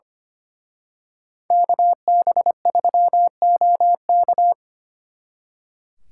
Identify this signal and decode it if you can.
Morse code, heard as a keyed tone, about 31 wpm, 700 Hz. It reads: KB3OK
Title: Speed Racing - USA 2x2 Call Signs 25wpm